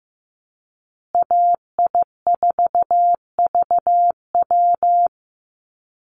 Morse code, heard as a keyed tone, about 15 wpm, 700 Hz. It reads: AI4VW
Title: USA Call Signs Repeated 3x 15wpm